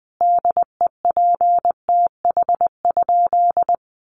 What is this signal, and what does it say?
Morse code, heard as a keyed tone, about 20 wpm, 700 Hz. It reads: DEPTH?
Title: Sentences from Top 1200 Words 20wpm